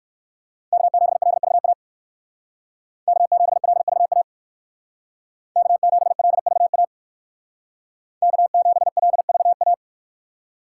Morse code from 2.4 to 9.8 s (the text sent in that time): K7LVA K7LVA K7LVA